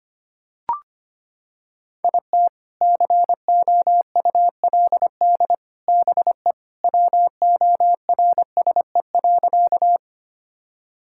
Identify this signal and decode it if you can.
Morse code, heard as a keyed tone, about 25 wpm, 700 Hz. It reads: IT COULD BE WORSE.
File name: practice / 25wpm / Sentences with 3-Word Phrases - 25wpm